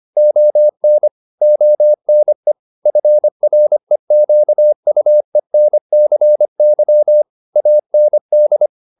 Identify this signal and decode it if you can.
Morse code, heard as a keyed tone, about 25 wpm, 600 Hz. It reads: ON ONE FREQUENCY AND